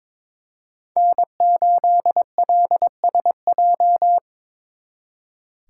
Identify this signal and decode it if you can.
Morse code, heard as a keyed tone, about 26 wpm, 700 Hz. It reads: N8LSJ
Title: Speed Racing - USA Call Signs 22wpm